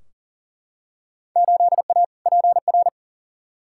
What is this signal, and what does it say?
Morse code, heard as a keyed tone, about 40 wpm, 700 Hz. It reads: 8A PR
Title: Field Day Exchanges Encore - 40wpm